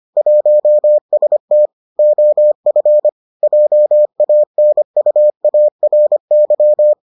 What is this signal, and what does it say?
Morse code, heard as a keyed tone, about 25 wpm, 600 Hz. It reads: 1ST OF JANUARY